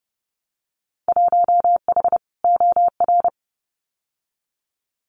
Morse code, heard as a keyed tone, about 30 wpm, 700 Hz. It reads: 1H OR
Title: Winter Field Day Exchanges 30wpm